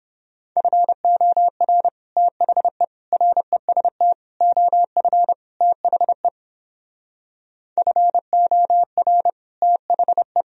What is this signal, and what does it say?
Morse code, heard as a keyed tone, about 31 wpm, 700 Hz. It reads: FOR THE REST OF THE FOR THE